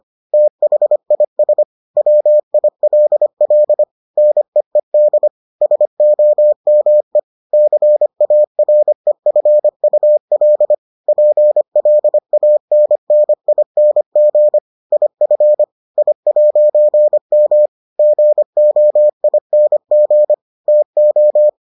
Morse code, heard as a keyed tone, about 25 wpm, 600 Hz. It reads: THIS WILL NEED SOME CAREFUL PLANNING IF I'M GOING TO